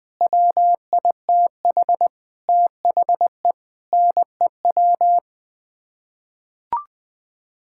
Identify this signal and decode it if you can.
Morse code, heard as a keyed tone, about 20 wpm, 700 Hz. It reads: WITH THE NEW
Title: Sets of 9 Words - 20wpm